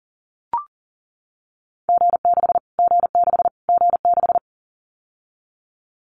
Morse code, heard as a keyed tone, about 40 wpm, 700 Hz. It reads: G6 G6 G6
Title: Callsign Prefixes for Top 25 DXCC - Repeated 3x - 40wpm